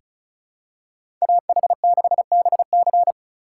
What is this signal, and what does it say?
Morse code, heard as a keyed tone, about 36 wpm, 700 Hz. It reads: AH6BC
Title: USA Call Signs - 5 Characters - 3x Character Spacing - 35wpm